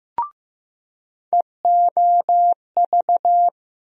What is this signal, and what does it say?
Morse code, heard as a keyed tone, about 15 wpm, 700 Hz. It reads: EOV
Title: USA Call Sign Postfixes - 3 Letters 15wpm